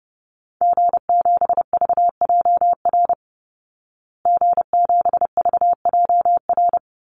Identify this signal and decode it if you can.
Morse code, heard as a keyed tone, about 30 wpm, 700 Hz. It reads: G7VJR G7VJR
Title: Call Signs Repeated 3x - 30wpm